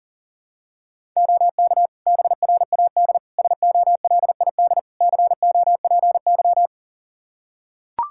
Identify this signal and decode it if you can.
Morse code, heard as a keyed tone, about 40 wpm, 700 Hz. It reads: OK BRAD SOLID COPY
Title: K1USN SST QSO Practice 40wpm